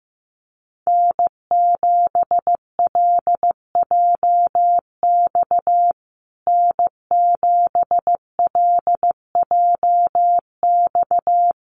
Morse code, heard as a keyed tone, about 15 wpm, 700 Hz. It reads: N7LJX N7LJX
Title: USA Call Signs Repeated 2x 15wpm